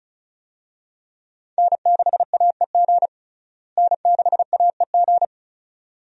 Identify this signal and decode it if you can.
Morse code, heard as a keyed tone, about 36 wpm, 700 Hz. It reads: N6AEG N6AEG
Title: USA Call Signs - 5 Characters - Repeated 2x - 35wpm